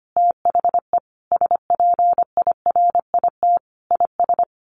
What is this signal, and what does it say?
Morse code, heard as a keyed tone, about 25 wpm, 700 Hz. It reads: THE SPIRIT IS
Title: Sentences from Top 1500 Words 25wpm